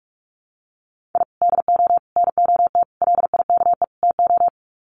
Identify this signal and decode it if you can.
Morse code, heard as a keyed tone, about 44 wpm, 700 Hz. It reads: I DO NOT LIKE TO
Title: Sets of 5 Words 45wpm